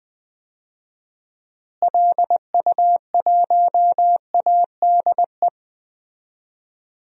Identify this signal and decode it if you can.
Morse code, heard as a keyed tone, about 20 wpm, 700 Hz. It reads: LU1ADE
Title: DXCC-100 Call Signs - 6 Characters - Repeated 2x - 20wpm